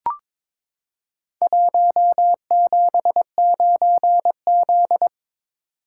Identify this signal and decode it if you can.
Morse code, heard as a keyed tone, about 22 wpm, 700 Hz. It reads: 179Z
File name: practice / 22wpm / Four Letter-Number Combinations - 22wpm